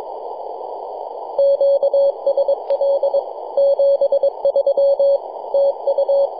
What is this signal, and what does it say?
Morse code, heard as a keyed tone, about 22 wpm, 560 Hz, with radio noise. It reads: QSL 73 TU
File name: yj0nk-18-075mhz-cw